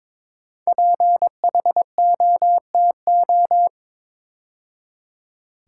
Morse code, heard as a keyed tone, about 22 wpm, 700 Hz. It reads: PHOTO
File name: 5 Letter Words - First 3 Letters Given - 22wpm